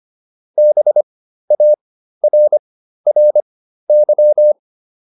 Morse code, heard as a keyed tone, about 25 wpm, spaced as for 13 wpm, 600 Hz. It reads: BARRY